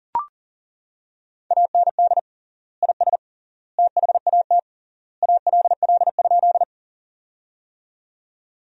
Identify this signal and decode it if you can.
Morse code, heard as a keyed tone, about 39 wpm, 700 Hz. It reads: AND IS THAT ALL?
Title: 3x Word-Spacing - Sentences from Top 100 Words - 40wpm